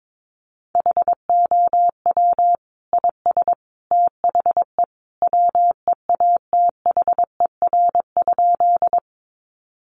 Morse code, heard as a keyed tone, about 22 wpm, 700 Hz. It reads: HOW IS THE WEATHER?
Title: Sentences from Top 800 Words 22wpm